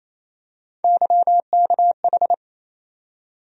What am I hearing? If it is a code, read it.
Morse code, heard as a keyed tone, about 28 wpm, 700 Hz. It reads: YKH
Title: USA Call Sign Postfixes - 3 Letters 28wpm